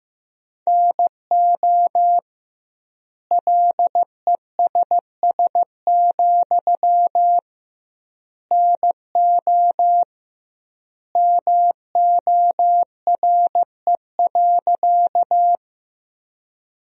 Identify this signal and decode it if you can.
Morse code, heard as a keyed tone, about 15 wpm, 700 Hz. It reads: NO LESS, NO MORE.
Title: 2x Word-Spacing - Sentences from Top 400 Words - 15wpm